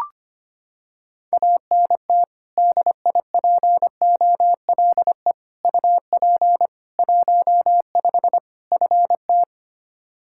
Morse code, heard as a keyed tone, about 25 wpm, 700 Hz. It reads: ANT DIPOLE UP 15 FT
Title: Practice QSO - Exchange 2 - Info - 25wpm